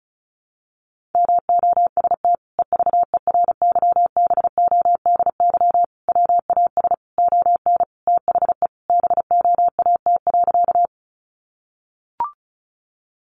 Morse code, heard as a keyed tone, about 34 wpm, 700 Hz. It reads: MOST EVERYBODY WAS ON THE BOAT.
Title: Sentences from Top 1400 Words 35wpm